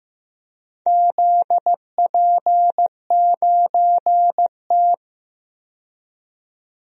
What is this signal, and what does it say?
Morse code, heard as a keyed tone, about 15 wpm, 700 Hz. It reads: ZP9T
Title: DXCC-100 Call Signs - 4 Characters - 15wpm